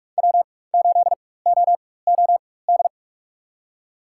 Morse code, heard as a keyed tone, about 44 wpm, spaced as for 21 wpm, 700 Hz. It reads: W8OOD
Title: USA Call Signs - 5 Characters - 4x Character Spacing - 45wpm